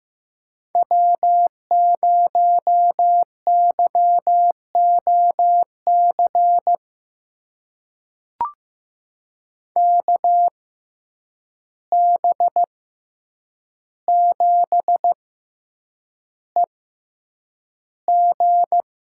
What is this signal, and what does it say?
Morse code, heard as a keyed tone, about 15 wpm, spaced as for 8 wpm, 700 Hz. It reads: W0YOC KB7EG
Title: USA Call Signs - 6x Character Spacing - 15wpm